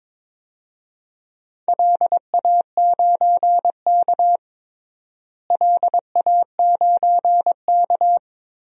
Morse code, heard as a keyed tone, about 22 wpm, 700 Hz. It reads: LA9K LA9K
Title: DXCC-100 Call Signs - 4 Characters - Repeated 2x - 22wpm